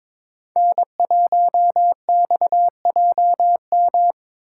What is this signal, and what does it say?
Morse code, heard as a keyed tone, about 22 wpm, 700 Hz. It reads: N1XJM
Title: K1USN SST QSO Practice - 22wpm